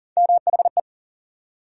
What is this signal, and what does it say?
Morse code, heard as a keyed tone, about 40 wpm, 700 Hz. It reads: MHE
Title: USA Call Sign Postfixes - 3 Letters 40wpm